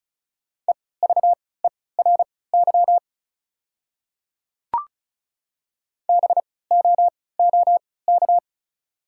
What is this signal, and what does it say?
Morse code, heard as a keyed tone, about 36 wpm, spaced as for 20 wpm, 690 Hz. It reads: EVERY BOOK
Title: 3x Character Spacing - Top 300 Words - 35wpm